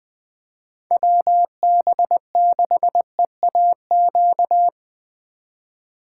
Morse code, heard as a keyed tone, about 20 wpm, 700 Hz. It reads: WB6EAQ
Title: Speed Racing - USA 2x3 Call Signs 17wpm